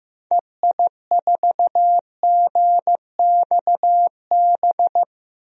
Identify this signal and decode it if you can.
Morse code, heard as a keyed tone, about 15 wpm, 700 Hz. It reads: EI4GXB
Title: DXCC-100 Call Signs - 6 Characters - 15wpm